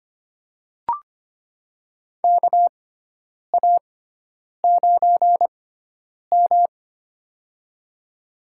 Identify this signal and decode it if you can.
Morse code, heard as a keyed tone, about 25 wpm, spaced as for 9 wpm, 700 Hz. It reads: KA9M
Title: USA Call Signs - 4 Characters - 6x Character Spacing - 25wpm